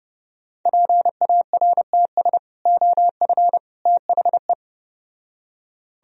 Morse code, heard as a keyed tone, about 33 wpm, 700 Hz. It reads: PARTS OF THE